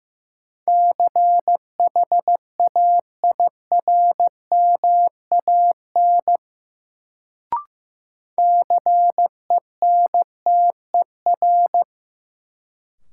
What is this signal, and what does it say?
Morse code, heard as a keyed tone, about 15 wpm, 700 Hz. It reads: CHAIRMAN CENTER
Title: Sets of 3 Words - New Word Drill - 15wpm